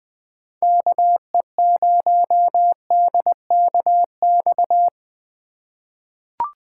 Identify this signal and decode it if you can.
Morse code, heard as a keyed tone, about 20 wpm, 700 Hz. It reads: KE0DKX
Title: USA Call Signs - 6 Characters - 3x Character Spacing - 20wpm